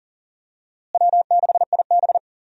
Morse code, heard as a keyed tone, about 40 wpm, 700 Hz. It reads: W6IB
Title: USA Call Signs - 4 Characters - Repeated 3x - 40wpm